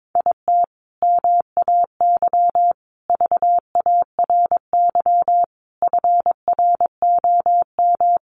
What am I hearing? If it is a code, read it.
Morse code, heard as a keyed tone, about 22 wpm, 700 Hz. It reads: IT MAY VARY FROM
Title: Sentences from Top 1200 Words 22wpm